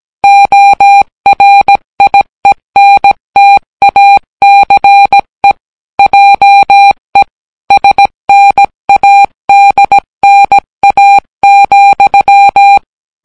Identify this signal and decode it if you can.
Morse code, heard as a keyed tone, about 17 wpm, 800 Hz. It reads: ORIENTACE JE SNADNA,